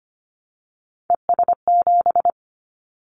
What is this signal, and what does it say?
Morse code, heard as a keyed tone, about 25 wpm, 700 Hz. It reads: ES7
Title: Callsign Prefixes for Top 100 DXCC - 25wpm